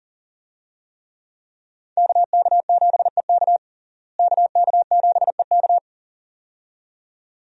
Morse code, heard as a keyed tone, about 40 wpm, 700 Hz. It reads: KK7EK KK7EK
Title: USA Call Signs - 5 Characters - Repeated 2x - 40wpm